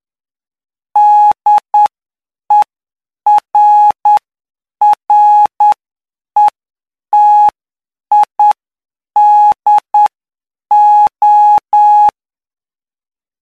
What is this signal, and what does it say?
Morse code, heard as a keyed tone, about 9 wpm, spaced as for 7 wpm, 840 Hz. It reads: DERRETIDO
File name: cw / py2lcd / 04-PY2LCD-Li��o 13